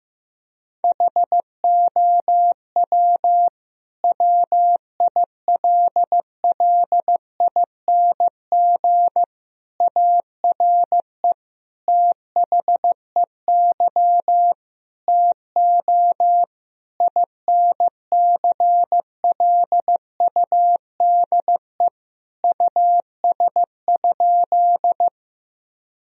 Morse code, heard as a keyed tone, about 15 wpm, 700 Hz. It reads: HOW WILLING ARE THEY TO INCLUDE US?
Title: Sentences from Top 1000 Words - Review Part 3 - 15wpm